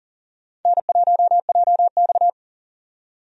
Morse code, heard as a keyed tone, about 40 wpm, 700 Hz. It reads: N1JX
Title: USA Call Signs - 4 Characters - 40wpm